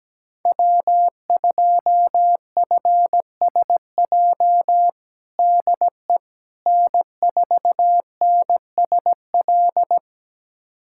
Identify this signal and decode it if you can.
Morse code, heard as a keyed tone, about 17 wpm, 700 Hz. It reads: W2FSJ DE N4NSL
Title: USA Callsign DE Callsign - 5 Characters - 17wpm